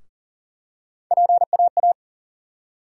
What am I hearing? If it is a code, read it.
Morse code, heard as a keyed tone, about 40 wpm, 700 Hz. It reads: PAA
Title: USA Call Sign Postfixes - 3 Letters 40wpm